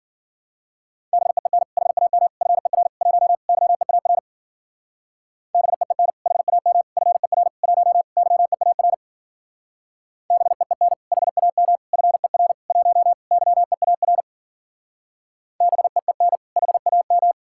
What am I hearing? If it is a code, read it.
Morse code, heard as a keyed tone, about 50 wpm, 700 Hz. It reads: BEEN HAM FER 1 YEAR BEEN HAM FER 1 YEAR BEEN HAM FER 1 YEAR BEEN HAM